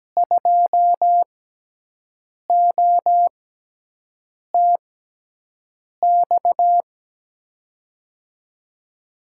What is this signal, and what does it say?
Morse code, heard as a keyed tone, about 17 wpm, spaced as for 6 wpm, 700 Hz. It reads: 2OTX
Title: USA Call Signs - 6 Characters - 6x Character Spacing - 17wpm